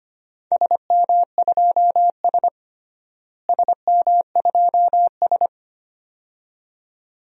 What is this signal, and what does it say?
Morse code, heard as a keyed tone, about 25 wpm, 700 Hz. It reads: SM2S SM2S
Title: DXCC-100 Call Signs - 4 Characters - Repeated 2x - 25wpm